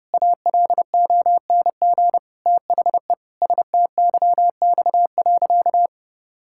Morse code, heard as a keyed tone, about 30 wpm, 700 Hz. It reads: ALONG THE STYX.
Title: Pangram Challenge v2 - 30wpm